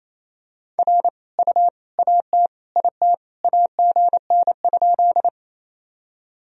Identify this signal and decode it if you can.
Morse code, heard as a keyed tone, about 28 wpm, 700 Hz. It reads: R U AT IT AGN?
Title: Sentences from Top 200 Words and Abbreviations 28wpm